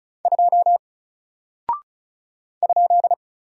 Morse code, heard as a keyed tone, about 35 wpm, 700 Hz. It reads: 2 ?
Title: Single Character (Letter:Number:Punctuation:Prosign) 35wpm